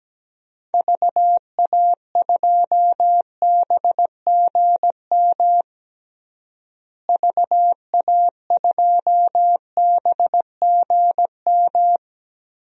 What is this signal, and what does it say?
Morse code, heard as a keyed tone, about 17 wpm, 700 Hz. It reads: VA2BGM VA2BGM
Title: DXCC-100 Call Signs - 6 Characters - Repeated 2x - 17wpm